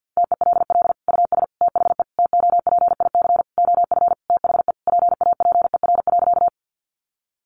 Morse code, heard as a keyed tone, about 50 wpm, 700 Hz. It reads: TELL US THE TOPIC OF THE PAPER.
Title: Sentences from Top 1200 Words 50wpm